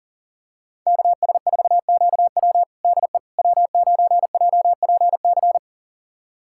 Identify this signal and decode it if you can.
Morse code, heard as a keyed tone, about 40 wpm, 700 Hz. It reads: KS4QW DE W9JPC
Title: USA Callsign DE Callsign - 5 Characters - 40wpm